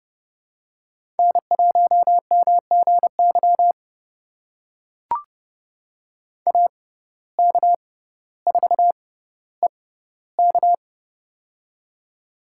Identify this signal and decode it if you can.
Morse code, heard as a keyed tone, about 30 wpm, spaced as for 15 wpm, 700 Hz. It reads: N1MGY AK4EK
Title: USA Call Signs - 5 Characters - 6x Character Spacing - 30wpm